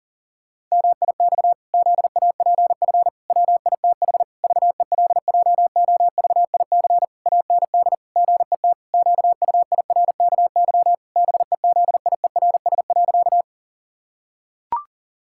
Morse code, heard as a keyed tone, about 40 wpm, 700 Hz. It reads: MIX ZAPF WITH VELJOVIC AND GET QUIRKY BEZIERS.
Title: Pangram Challenge v2 - 40wpm